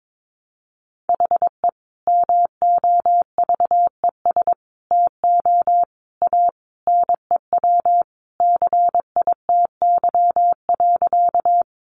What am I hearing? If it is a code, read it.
Morse code, heard as a keyed tone, about 22 wpm, 700 Hz. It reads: HE MOVES TO A NEW CITY.
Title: Easy Phrases 22wpm